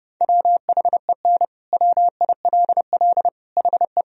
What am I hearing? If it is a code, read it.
Morse code, heard as a keyed tone, about 30 wpm, 700 Hz. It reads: WHEN WILL HE
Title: Sentences from Top 1000 Words - Review Part 1 - 30wpm